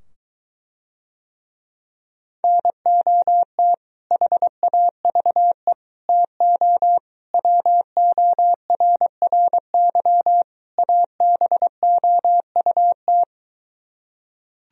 Morse code, heard as a keyed tone, about 24 wpm, 700 Hz. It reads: NOT HAVE TO WORRY ABOUT